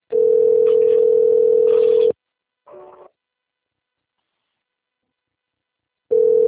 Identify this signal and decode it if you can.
Ringback tone (North America), 440 + 480 Hz.